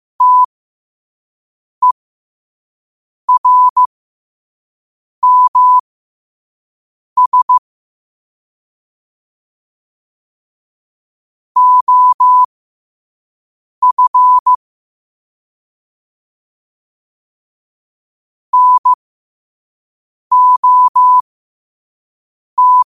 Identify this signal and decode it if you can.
Morse code, heard as a keyed tone, about 15 wpm, spaced as for 5 wpm, 1000 Hz. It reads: TERMS OF NOT
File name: qotdfiles / 5wpm